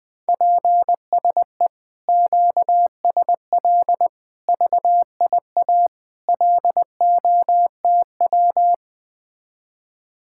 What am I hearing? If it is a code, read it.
Morse code, heard as a keyed tone, about 20 wpm, 700 Hz. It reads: PSE QSL VIA LOTW
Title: Speed-Racing Common QSO Phrases - 17wpm